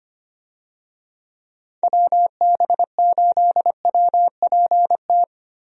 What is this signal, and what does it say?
Morse code, heard as a keyed tone, about 25 wpm, 700 Hz. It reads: WB8WPT
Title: USA Call Signs - 6 Characters - Repeated 3x - 25wpm